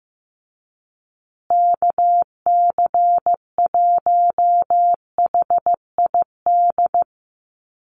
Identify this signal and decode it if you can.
Morse code, heard as a keyed tone, about 15 wpm, 700 Hz. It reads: KC1HID
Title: USA Call Signs Repeated 3x 15wpm